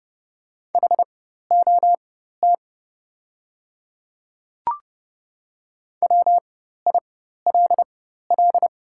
Morse code, heard as a keyed tone, about 30 wpm, spaced as for 14 wpm, 700 Hz. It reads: HOT WILL
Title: 4x Character Spacing - Top 100 Words - 30wpm